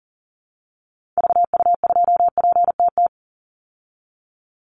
Morse code, heard as a keyed tone, about 40 wpm, 700 Hz. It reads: VU2PTT
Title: Call Signs Repeated 3x - 40wpm